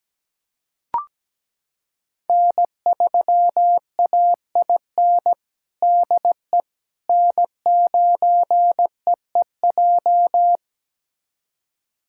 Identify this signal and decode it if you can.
Morse code, heard as a keyed tone, about 17 wpm, 700 Hz. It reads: N3AIN DE N9EEJ
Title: USA Callsign DE Callsign 17wpm